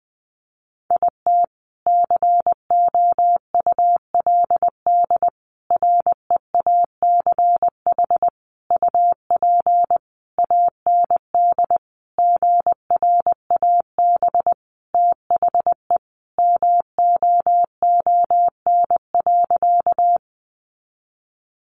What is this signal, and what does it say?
Morse code, heard as a keyed tone, about 20 wpm, 700 Hz. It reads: IT COULD REACH UP AND GRAB THE MOON.
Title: A Sound Of Thunder 20wpm